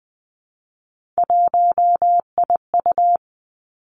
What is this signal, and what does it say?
Morse code, heard as a keyed tone, about 20 wpm, 700 Hz. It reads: 1IU
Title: Three Challenging Letter-Number Combinations 20wpm